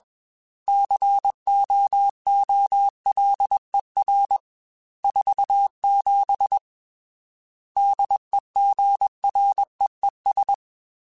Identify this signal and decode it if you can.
Morse code, heard as a keyed tone, about 21 wpm, 800 Hz. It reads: COOLER 47 DEGREES
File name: Weather Sets / Weather Sets-mp3s / Weather 5 5 @20